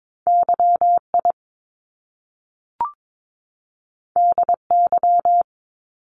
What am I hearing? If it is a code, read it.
Morse code, heard as a keyed tone, about 22 wpm, 700 Hz. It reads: YI DY
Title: Top Two Letter Combinations in Words 22wpm